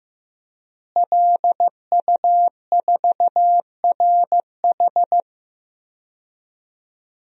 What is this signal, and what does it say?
Morse code, heard as a keyed tone, about 15 wpm, 700 Hz. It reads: LU4RH
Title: DXCC-100 Call Signs - 5 Characters - Repeated 2x - 15wpm